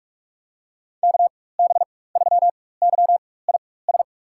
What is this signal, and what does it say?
Morse code, heard as a keyed tone, about 46 wpm, spaced as for 21 wpm, 700 Hz. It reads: KB3YIS
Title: USA Call Signs - 6 Characters - 4x Character Spacing - 45wpm